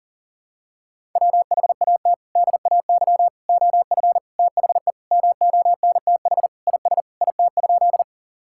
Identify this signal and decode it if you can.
Morse code, heard as a keyed tone, about 40 wpm, 700 Hz. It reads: WHAT DAY OF THE MONTH IS IT?
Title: Sentences from Top 1000 Words - Review Part 5 - 40wpm